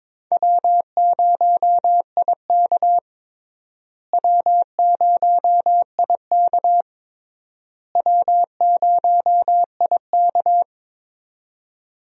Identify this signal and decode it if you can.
Morse code, heard as a keyed tone, about 21 wpm, 700 Hz. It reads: W0IK W0IK W0IK
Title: USA Call Signs - 4 Characters - Repeated 3x - 22wpm